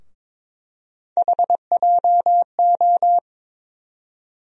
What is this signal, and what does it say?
Morse code, heard as a keyed tone, about 22 wpm, 700 Hz. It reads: HJO
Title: USA Call Sign Postfixes - 3 Letters 22wpm